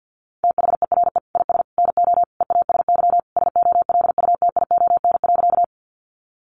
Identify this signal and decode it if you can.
Morse code, heard as a keyed tone, about 50 wpm, 700 Hz. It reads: THERE IS NO EASY SOLUTION.
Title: Sentences from Top 1300 Words 50wpm